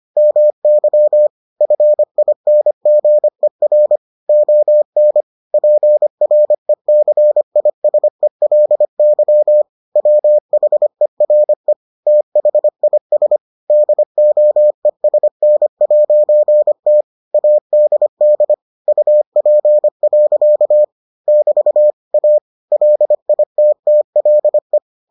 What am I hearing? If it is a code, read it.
Morse code, heard as a keyed tone, about 25 wpm, 600 Hz. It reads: MY FINGER ON PRECISELY WHERE THIS DOESN'T ADD UP. <BT> A LITTLE